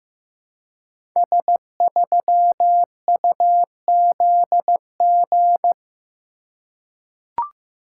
Morse code, heard as a keyed tone, about 15 wpm, 700 Hz. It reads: S3UZG
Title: Code Groups - 5 Letter:Numbers - 15wpm